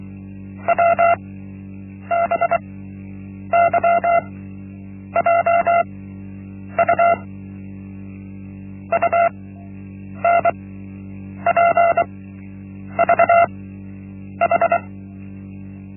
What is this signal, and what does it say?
Morse code, heard as a keyed tone, about 24 wpm, spaced as for 8 wpm, 680 Hz. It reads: WBYJU UNPVH
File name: Linksmakalnis-1